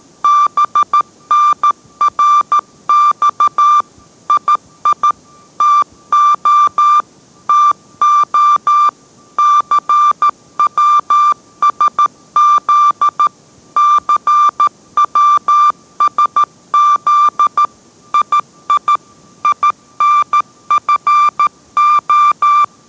Morse code, heard as a keyed tone, about 13 wpm, 1180 Hz. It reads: BNRX II TO TO CWSZ CWSZ II INFO